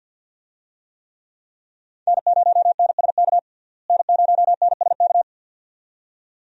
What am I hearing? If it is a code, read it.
Morse code, heard as a keyed tone, about 50 wpm, 700 Hz. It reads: N0NSK N0NSK
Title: USA Call Signs - 5 Characters - Repeated 2x - 50wpm